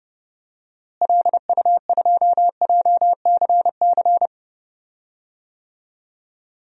Morse code, heard as a keyed tone, about 31 wpm, 700 Hz. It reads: LU2JCC
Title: DXCC-100 Call Signs - 6 Characters - Repeated 2x - 30wpm